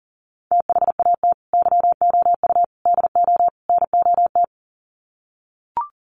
Morse code, heard as a keyed tone, about 40 wpm, 700 Hz. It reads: THAT YOU DO NOT
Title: Sets of 4 Words 40wpm